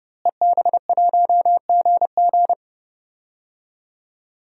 Morse code, heard as a keyed tone, about 31 wpm, 700 Hz. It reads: EB1GG
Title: DXCC-100 Call Signs - 5 Characters - Repeated 2x - 30wpm